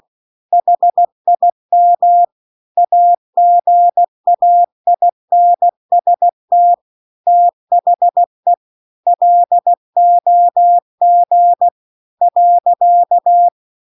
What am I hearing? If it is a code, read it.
Morse code, heard as a keyed tone, about 16 wpm, 700 Hz. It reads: HIM AGAINST THE LOG .